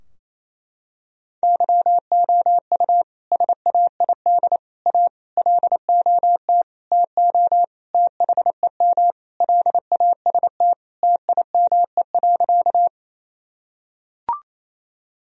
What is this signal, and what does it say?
Morse code, heard as a keyed tone, about 28 wpm, 700 Hz. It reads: YOU SAID A LOT TO THEM LAST TIME.
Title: Sentences from Top 1000 Words - Review Part 3 - 28wpm